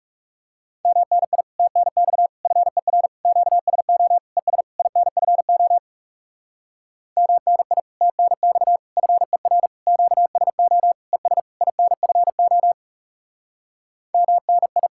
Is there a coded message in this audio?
Yes: Morse code from 0.9 to 14.9 s, heard as a keyed tone, about 38 wpm, 700 Hz. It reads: MNI TNX FER QSO ES INFO MNI TNX FER QSO ES INFO MNI